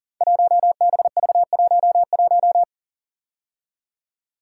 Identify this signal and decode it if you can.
Morse code, heard as a keyed tone, about 39 wpm, 690 Hz. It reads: 1BV11
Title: Code Groups - 5 Letter:Numbers - 40wpm